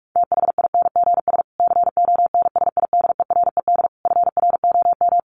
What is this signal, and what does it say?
Morse code, heard as a keyed tone, about 44 wpm, 700 Hz. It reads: THINGS CONSIDERED FROM
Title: Sets of 5 Words Encore 45wpm